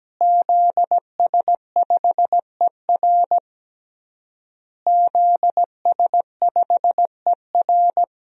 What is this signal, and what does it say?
Morse code, heard as a keyed tone, about 17 wpm, 700 Hz. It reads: ZS5ER ZS5ER
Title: DXCC-100 Call Signs - 5 Characters - Repeated 3x - 17wpm